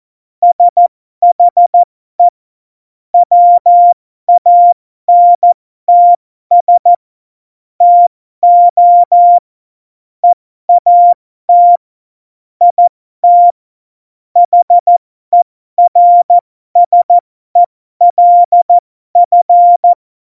Morse code, heard as a keyed tone, about 14 wpm, spaced as for 12 wpm, 700 Hz. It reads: SHE WANTS TO EAT IT HERSELF
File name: Morsekurs / Witze